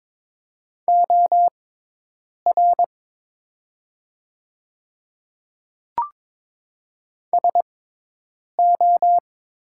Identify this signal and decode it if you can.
Morse code, heard as a keyed tone, about 21 wpm, spaced as for 8 wpm, 700 Hz. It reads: OR SO
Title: 6x Character Spacing - Top 100 Words - 22wpm